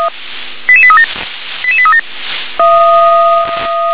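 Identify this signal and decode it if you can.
Five-tone selective call (ZVEI), heard as quick runs of five tones, tones of about 70 ms. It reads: 88837 88837
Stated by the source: ZVEI five-tone call received on SDR.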